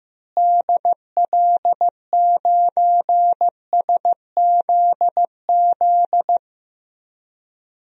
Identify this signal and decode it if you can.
Morse code, heard as a keyed tone, about 15 wpm, 700 Hz. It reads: DL9SZZ
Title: DXCC-100 Call Signs - 6 Characters - Repeated 2x - 15wpm